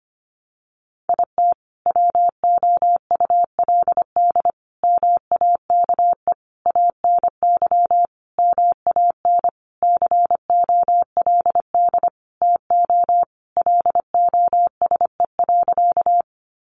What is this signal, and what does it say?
Morse code, heard as a keyed tone, about 25 wpm, 700 Hz. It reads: IT WOULD MAKE ANY MAN COLD TO LOSE.
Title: Sentences from Top 1100 Words 25wpm